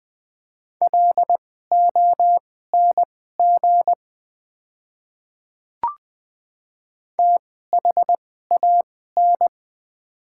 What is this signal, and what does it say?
Morse code, heard as a keyed tone, about 20 wpm, spaced as for 14 wpm, 700 Hz. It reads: LONG THAN
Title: 2x Character Spacing - Top 100 Words - 20wpm